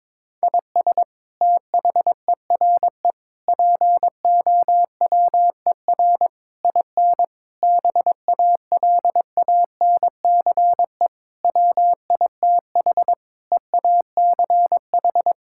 Morse code, heard as a keyed tone, about 21 wpm, 700 Hz. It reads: IS THERE POWER IN BALANCE WITH EACH
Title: Sentences from Top 1000 Words - Review Part 6 - 22wpm